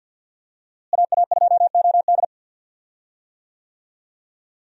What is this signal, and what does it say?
Morse code, heard as a keyed tone, about 47 wpm, 700 Hz. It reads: AA2OD
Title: USA Call Signs - 5 Characters - Repeated 3x - 50wpm